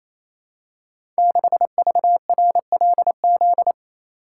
Morse code, heard as a keyed tone, about 28 wpm, 700 Hz. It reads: 6VRLZ
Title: Code Groups - 5 Letter:Numbers - 28wpm